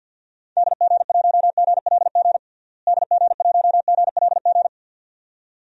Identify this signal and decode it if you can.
Morse code, heard as a keyed tone, about 47 wpm, 700 Hz. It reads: DG1GLG DG1GLG
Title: DXCC-100 Call Signs - 6 Characters - Repeated 2x - 50wpm